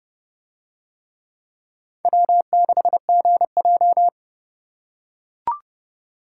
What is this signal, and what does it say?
Morse code, heard as a keyed tone, about 30 wpm, 700 Hz. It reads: W6GJ